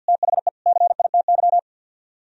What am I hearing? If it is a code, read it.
Morse code, heard as a keyed tone, about 64 wpm, 700 Hz. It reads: THE CITY